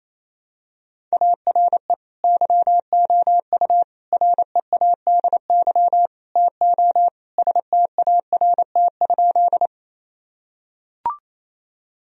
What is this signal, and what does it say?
Morse code, heard as a keyed tone, about 28 wpm, 700 Hz. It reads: ARE YOU READY TO START?
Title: Sentences from Top 1000 Words - Review Part 1 - 28wpm